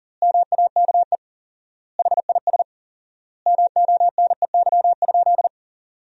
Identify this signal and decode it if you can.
Morse code, heard as a keyed tone, about 40 wpm, 700 Hz. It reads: MAKE HIS MONEY?
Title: 4x Word-Spacing - Sentences from Top 400 Words - 40wpm